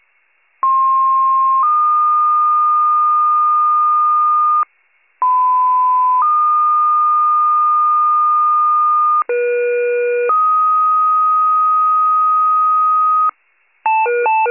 Two-tone page, heard as a one-second tone, then a three-second tone, about 1060 then 1220 Hz, about 1010 then 1220 Hz, about 480 then 1250 Hz. It is heard from 0.6 to 13.3 s.
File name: pager / 94 RESCUE / wcalert